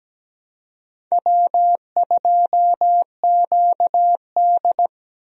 Morse code, heard as a keyed tone, about 17 wpm, 700 Hz. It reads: W2QD
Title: USA Call Signs - 4 Characters - Repeated 3x - 17wpm